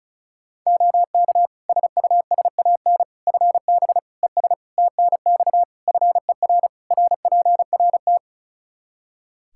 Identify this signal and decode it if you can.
Morse code, heard as a keyed tone, about 34 wpm, 700 Hz. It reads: OK SUSAN FB ES TNX FER RPRT
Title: Practice QSO - Exchange 2 - Info - 35wpm